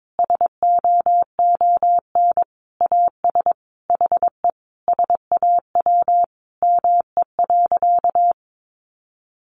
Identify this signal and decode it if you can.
Morse code, heard as a keyed tone, about 22 wpm, 700 Hz. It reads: SOON AS HE SAW ME.
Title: Sentences from Top 1100 Words 22wpm